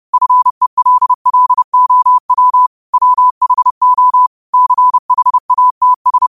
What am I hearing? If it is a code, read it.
Morse code, heard as a keyed tone, about 30 wpm, 1000 Hz. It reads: FELLOW WHO CHATS